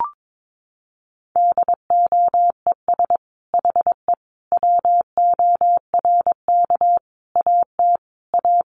Morse code, heard as a keyed tone, about 22 wpm, 700 Hz. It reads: DOES HE WORK AT A
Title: Sentences from Top 1300 Words 22wpm